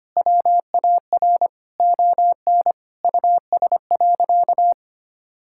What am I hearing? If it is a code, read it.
Morse code, heard as a keyed tone, about 25 wpm, 700 Hz. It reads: WAR ON US.
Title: Sentences from Top 1000 Words - Review Part 6 - 25wpm